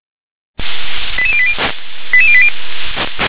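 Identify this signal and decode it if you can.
Five-tone selective call (ZVEI), heard as quick runs of five tones, tones of about 70 ms. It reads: 88080 88080
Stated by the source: ZVEI five-tone call received by SDR.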